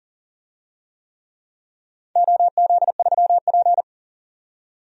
Morse code, heard as a keyed tone, about 40 wpm, 700 Hz. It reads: OZ3P
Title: DXCC-100 Call Signs - 4 Characters - 40wpm